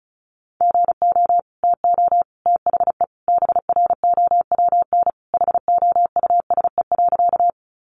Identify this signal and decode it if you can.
Morse code, heard as a keyed tone, about 34 wpm, 700 Hz. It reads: GO TO THE BROWN HOUSE.
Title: Sentences from Top 1200 Words 35wpm